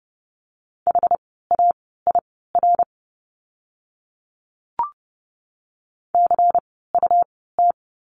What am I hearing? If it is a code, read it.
Morse code, heard as a keyed tone, about 30 wpm, spaced as for 17 wpm, 700 Hz. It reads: HAIR CUT
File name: Lesson 17f - Letter u - Words 30wpm